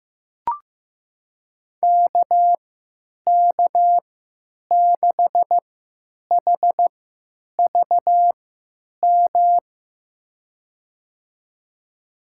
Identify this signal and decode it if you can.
Morse code, heard as a keyed tone, about 15 wpm, spaced as for 9 wpm, 700 Hz. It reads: KK6HVM
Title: USA Call Signs - 6 Characters - 3x Character Spacing - 15wpm